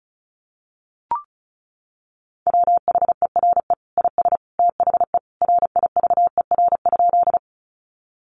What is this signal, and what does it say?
Morse code, heard as a keyed tone, about 34 wpm, 700 Hz. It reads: WHERE IS THE RIVER?
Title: Sentences from Top 300 Words 35wpm